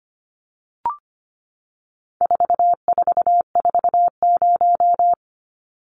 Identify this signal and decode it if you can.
Morse code, heard as a keyed tone, about 25 wpm, 700 Hz. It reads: 4440
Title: Numbers - 4-Digit 25wpm